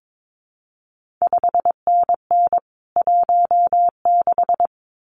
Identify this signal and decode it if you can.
Morse code, heard as a keyed tone, about 22 wpm, 700 Hz. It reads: 5NN 16
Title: CQ WW DX Contest Exchange - 22wpm